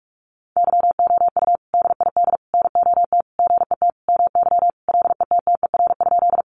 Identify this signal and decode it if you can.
Morse code, heard as a keyed tone, about 44 wpm, 700 Hz. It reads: YOU DID NOT GET MY LETTER?
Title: Sentences from Top 600 Words 45wpm